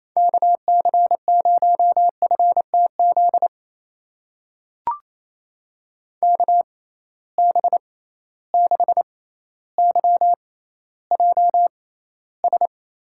Morse code, heard as a keyed tone, about 28 wpm, spaced as for 14 wpm, 700 Hz. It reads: KC0FTZ KB6YJS
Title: USA Call Signs - 6 Characters - 6x Character Spacing - 28wpm